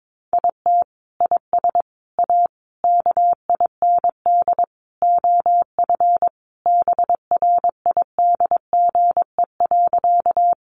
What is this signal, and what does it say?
Morse code, heard as a keyed tone, about 22 wpm, 700 Hz. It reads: IT IS A KIND OF BRIDGE.
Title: Sentences from Top 1200 Words 22wpm